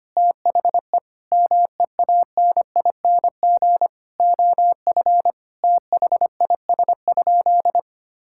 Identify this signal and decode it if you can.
Morse code, heard as a keyed tone, about 25 wpm, 700 Hz. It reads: THE MEANING OF THIS?
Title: Sentences from Top 1000 Words - Review Part 6 - 25wpm